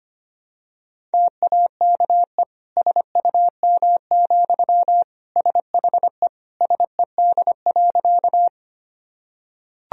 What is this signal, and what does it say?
Morse code, heard as a keyed tone, about 25 wpm, 700 Hz. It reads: TAKE SUM, SHE SED.
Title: Sentences from Top 200 Words and Abbreviations 25wpm